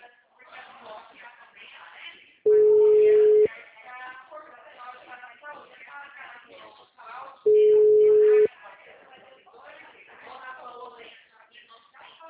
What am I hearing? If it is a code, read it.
Ringback tone (Europe, 425 Hz), with speech.